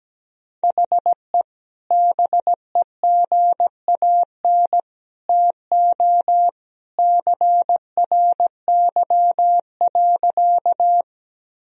Morse code, heard as a keyed tone, about 17 wpm, 700 Hz. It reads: HE BEGAN TO CRY.
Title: Sentences from Top 1000 Words - Review Part 2 - 17wpm